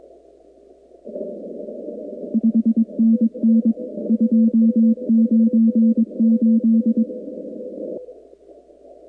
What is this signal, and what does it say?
Morse code, heard as a keyed tone, about 22 wpm, 230 Hz. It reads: 5NN 298
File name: contests / 2019-cw-wpx-kt5j-k5pi- -2330 0-45